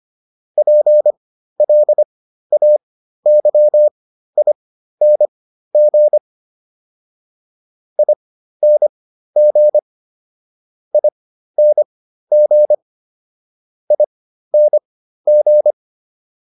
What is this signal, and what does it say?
Morse code, heard as a keyed tone, about 25 wpm, spaced as for 13 wpm, 600 Hz. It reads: PLAYING ING ING ING